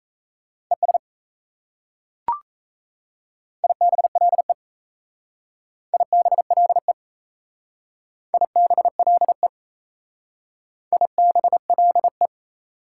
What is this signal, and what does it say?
Morse code, heard as a keyed tone, about 33 wpm, 700 Hz. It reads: ES IBLE IBLE IBLE IBLE
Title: Speed-Racing Prefixes and Suffixes 28wpm